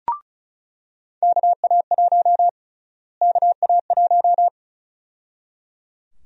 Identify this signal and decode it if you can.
Morse code, heard as a keyed tone, about 34 wpm, 700 Hz. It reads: KA1 KA1
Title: USA Call Sign Prefixes - Repeated 2x - 35wpm